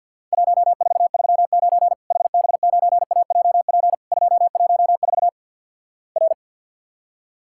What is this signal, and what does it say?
Morse code, heard as a keyed tone, about 52 wpm, 700 Hz. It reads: 1439 HB9AJP 214 R
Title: Worked All Europe DX Contest - QTC Exchange - 50wpm